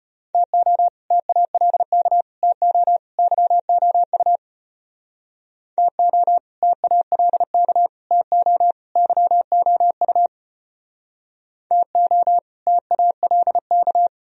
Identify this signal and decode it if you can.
Morse code, heard as a keyed tone, about 33 wpm, 700 Hz. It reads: TO TALK TO YOU TO TALK TO YOU TO TALK